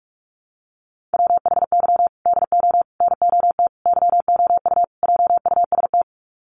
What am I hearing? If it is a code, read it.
Morse code, heard as a keyed tone, about 44 wpm, 700 Hz. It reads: WHY DO NOT YOU JUST
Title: Sets of 5 Words 45wpm